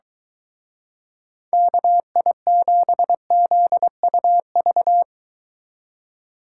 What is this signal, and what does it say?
Morse code, heard as a keyed tone, about 23 wpm, 700 Hz. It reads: KI7ZUV